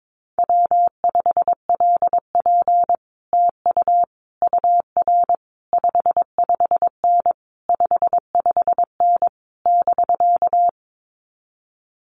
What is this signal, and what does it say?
Morse code, heard as a keyed tone, about 22 wpm, 700 Hz. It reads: W5LP TU UR 55N 55N <BK>
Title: Common QSO Phrases Encore 22wpm